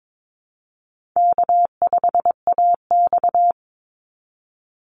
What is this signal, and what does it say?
Morse code, heard as a keyed tone, about 22 wpm, 700 Hz. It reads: K5AX
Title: Easy Call Signs - 22wpm